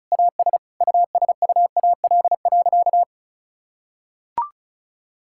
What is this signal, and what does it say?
Morse code, heard as a keyed tone, about 36 wpm, 700 Hz. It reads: AS USUAL.
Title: Sentences from Top 1000 Words - Review Part 6 - 35wpm